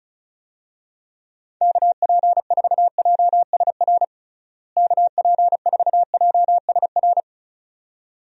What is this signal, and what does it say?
Morse code, heard as a keyed tone, about 36 wpm, 700 Hz. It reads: KP4JSR KP4JSR
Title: USA Call Signs - 6 Characters - Repeated 2x - 35wpm